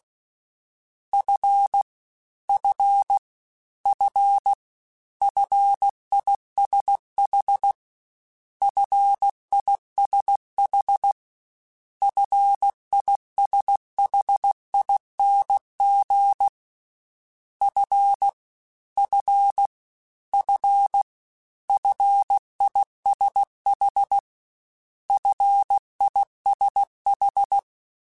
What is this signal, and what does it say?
Morse code, heard as a keyed tone, about 16 wpm, 800 Hz. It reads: F F F FISH FISH FISHING F F F FISH FISH
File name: Combinations / Combinations-mp3s / Combinations@15